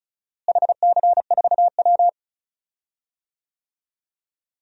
Morse code, heard as a keyed tone, about 36 wpm, 700 Hz. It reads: HC4W
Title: DXCC-100 Call Signs - 4 Characters - Repeated 3x - 35wpm